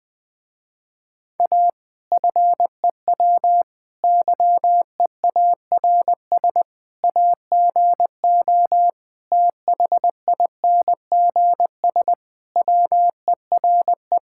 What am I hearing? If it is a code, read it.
Morse code, heard as a keyed tone, about 21 wpm, 700 Hz. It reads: A FEW YEARS AGO THINGS WERE
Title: Sentences with 4-Word Phrases - Spoken Beginning - 20wpm